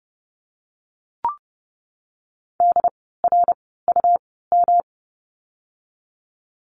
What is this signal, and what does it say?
Morse code, heard as a keyed tone, about 30 wpm, spaced as for 17 wpm, 700 Hz. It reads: DRUM
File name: course / 30:10wpm / 108 Lesson 18f - Letter m - Words 30wpm